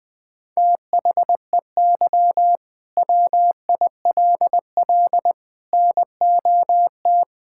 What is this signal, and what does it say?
Morse code, heard as a keyed tone, about 20 wpm, 700 Hz. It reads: THEY WILL NOT
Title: Sentences from Top 1000 Words - Review Part 5 - 20wpm